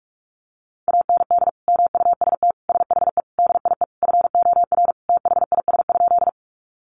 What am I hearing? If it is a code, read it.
Morse code, heard as a keyed tone, about 44 wpm, 700 Hz. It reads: AND MUST SHE DIE FOR THIS?
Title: Sentences from Top 1300 Words 45wpm